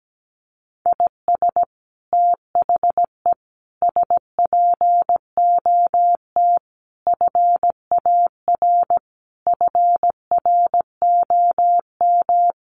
Morse code, heard as a keyed tone, about 17 wpm, 700 Hz. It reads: IS THE SPOT FAR FROM
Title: Sentences from Top 1100 Words 17wpm